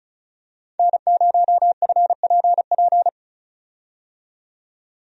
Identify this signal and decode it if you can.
Morse code, heard as a keyed tone, about 36 wpm, 700 Hz. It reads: N0FPP
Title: USA Call Signs - 5 Characters - 35wpm